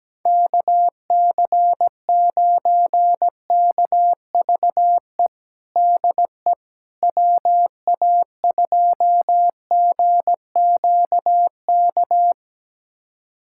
Morse code, heard as a keyed tone, about 17 wpm, 700 Hz. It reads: KC9KVE DE WA2GQK
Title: USA Callsign DE Callsign - 6 Characters - 17wpm